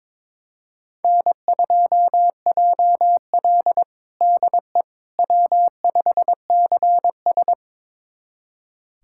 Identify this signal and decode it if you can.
Morse code, heard as a keyed tone, about 22 wpm, 700 Hz. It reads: N2JL DE W5CS
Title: USA Callsign DE Callsign - 4 Characters - 22wpm